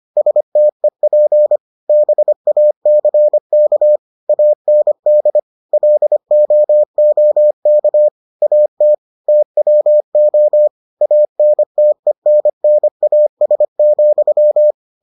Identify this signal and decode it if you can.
Morse code, heard as a keyed tone, about 25 wpm, 600 Hz. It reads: STEP BACK AND LOOK AT TWO ANTENNAS,